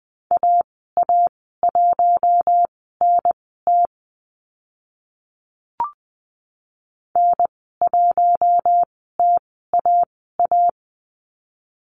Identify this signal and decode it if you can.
Morse code, heard as a keyed tone, about 20 wpm, spaced as for 15 wpm, 700 Hz. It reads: AA1NT N1TAA
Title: 055 Lesson 10g - Letter r - Call Signs 20wpm